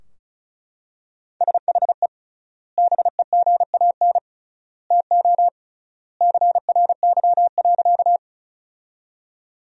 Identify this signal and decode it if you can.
Morse code, heard as a keyed tone, about 34 wpm, 700 Hz. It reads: SHE BEGAN TO CRY.
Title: 3x Word-Spacing - Sentences from Top 500 Words - 35wpm